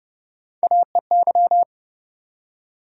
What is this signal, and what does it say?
Morse code, heard as a keyed tone, about 30 wpm, 700 Hz. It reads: AEY
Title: USA Call Sign Postfixes - 3 Letters 30wpm